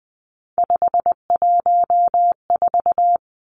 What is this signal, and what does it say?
Morse code, heard as a keyed tone, about 20 wpm, 700 Hz. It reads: 514
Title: Three Challenging Letter-Number Combinations 20wpm